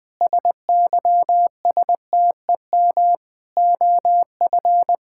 Morse code, heard as a keyed tone, about 20 wpm, 700 Hz. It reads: SYSTEM OF
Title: Sentences from Top 1000 Words - Review Part 5 - 20wpm